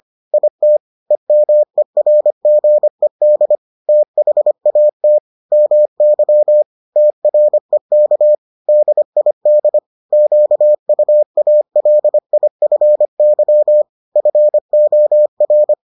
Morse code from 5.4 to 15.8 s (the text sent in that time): MY TREK DID QUALIFY FOR